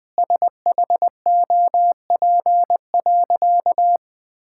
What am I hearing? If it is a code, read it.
Morse code, heard as a keyed tone, about 21 wpm, 700 Hz. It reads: SHOP.
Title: Sentences from Top 1600 Words 20wpm